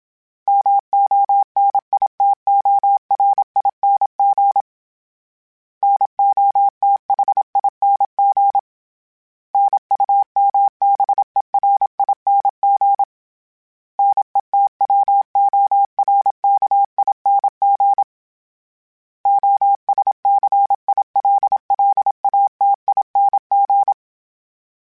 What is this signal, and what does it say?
Morse code, heard as a keyed tone, about 27 wpm, 800 Hz. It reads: MONITORING NOTHING NUMBERING NETWORKING OSCILLATING
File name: Special INGs-mp3s / Special INGs-mp3s / Sngl Ing @25